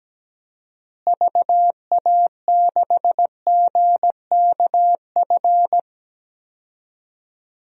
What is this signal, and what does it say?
Morse code, heard as a keyed tone, about 17 wpm, 700 Hz. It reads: VA6GKF
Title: DXCC-100 Call Signs - 6 Characters - Repeated 3x - 17wpm